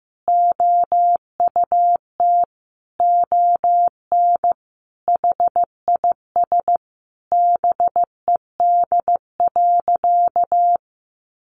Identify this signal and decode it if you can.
Morse code, heard as a keyed tone, about 15 wpm, 700 Hz. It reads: OUT ON HIS BED.
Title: Sentences from Top 1200 Words 15wpm